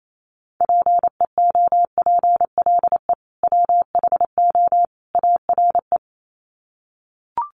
Morse code, heard as a keyed tone, about 28 wpm, 700 Hz. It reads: PEOPLE WHO ARE
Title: Sets of 3 Words 28wpm